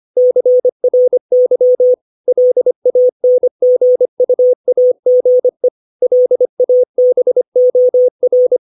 Morse code, heard as a keyed tone, about 25 wpm, 500 Hz. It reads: CRY LANGUAGE LABOR
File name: 632 words / 632 Words 25WPM